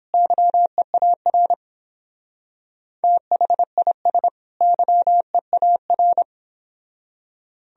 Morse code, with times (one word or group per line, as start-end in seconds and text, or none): 0.1-1.5 YEAR
3.0-4.3 THIS
4.6-6.2 YEAR